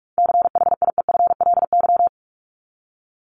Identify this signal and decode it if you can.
Morse code, heard as a keyed tone, about 44 wpm, 700 Hz. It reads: CHIEFLY
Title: LY Suffix - Top 100 Words with Base Words 45wpm